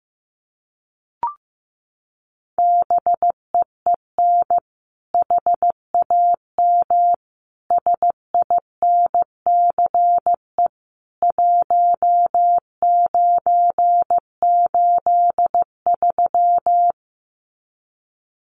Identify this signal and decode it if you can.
Morse code, heard as a keyed tone, about 15 wpm, 700 Hz. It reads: BEEN HAM SINCE 1983
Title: Common QSO Phrases 15wpm